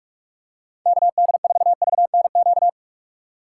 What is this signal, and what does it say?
Morse code, heard as a keyed tone, about 44 wpm, 700 Hz. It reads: KD4VNQ
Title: USA Call Signs - 3x Character Spacing - 45wpm